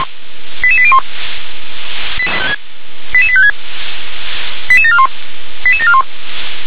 Five-tone selective call (ZVEI), heard as quick runs of five tones, tones of about 70 ms. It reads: 88801 88856 88741 88741